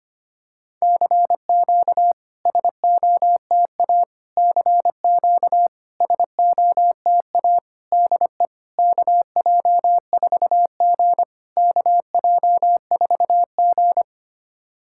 Morse code, heard as a keyed tone, about 25 wpm, 700 Hz. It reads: CQ SOTA CQ SOTA DE KJ4G KJ4G
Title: SOTA QSO Practice 25wpm